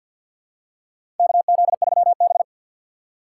Morse code, heard as a keyed tone, about 49 wpm, 700 Hz. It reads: KZ3B
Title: USA Call Signs - 4 Characters - 2x Character Spacing - 50wpm